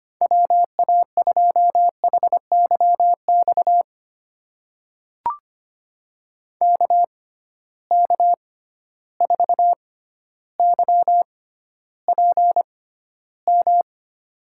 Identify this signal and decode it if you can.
Morse code, heard as a keyed tone, about 25 wpm, spaced as for 13 wpm, 700 Hz. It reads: WA2HYX KK4YPM
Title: USA Call Signs - 6 Characters - 6x Character Spacing - 25wpm